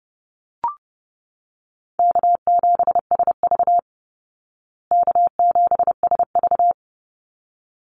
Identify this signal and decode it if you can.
Morse code, heard as a keyed tone, about 30 wpm, 700 Hz. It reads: K7SV K7SV
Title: Easy Call Signs Repeated 2x - 30wpm